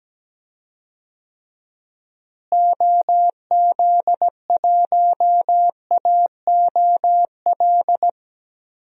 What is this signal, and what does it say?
Morse code, heard as a keyed tone, about 17 wpm, 700 Hz. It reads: OZ1AOL
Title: DXCC-100 Call Signs - 6 Characters - Repeated 2x - 17wpm